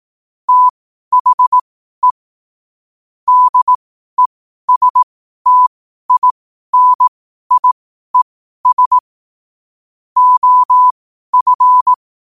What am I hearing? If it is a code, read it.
Morse code, heard as a keyed tone, about 18 wpm, spaced as for 12 wpm, 1000 Hz. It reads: THE DESTINIES OF
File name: qotdfiles / 12wpm